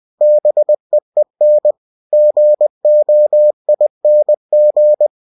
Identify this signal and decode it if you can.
Morse code, heard as a keyed tone, about 20 wpm, 600 Hz. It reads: BEEN GOING
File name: cw-books / winnie-the-pooh / winnie-20wpm-page-0022